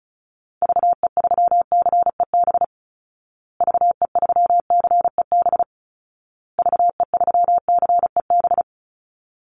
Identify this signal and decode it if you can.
Morse code, heard as a keyed tone, about 34 wpm, 700 Hz. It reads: VE3CEB VE3CEB VE3CEB
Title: Call Signs Repeated 2x - 35wpm